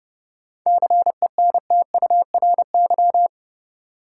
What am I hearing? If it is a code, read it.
Morse code, heard as a keyed tone, about 32 wpm, 700 Hz. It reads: CENTURY
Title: Sets of 5 Words - New Word Drill - 30wpm